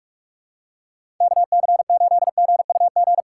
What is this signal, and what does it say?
Morse code, heard as a keyed tone, about 48 wpm, 700 Hz. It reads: KC8GUG
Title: USA Call Signs - 6 Characters - 6x Character Spacing - 45wpm